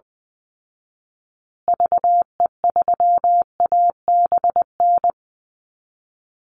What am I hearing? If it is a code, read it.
Morse code, heard as a keyed tone, about 20 wpm, 700 Hz. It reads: VE3ABN
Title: Call Signs Repeated 2x - 20wpm